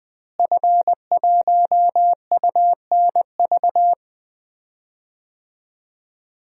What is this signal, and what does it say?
Morse code, heard as a keyed tone, about 21 wpm, 700 Hz. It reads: F1UNV
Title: DXCC-100 Call Signs - 5 Characters - Repeated 3x - 20wpm